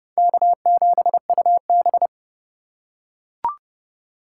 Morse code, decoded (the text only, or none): K7UB